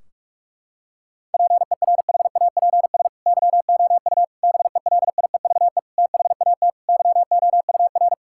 Morse code, heard as a keyed tone, about 41 wpm, 700 Hz. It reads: PERHAPS YOU BELIEVE THAT YOUR
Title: Sentences from Top 1600 Words 45wpm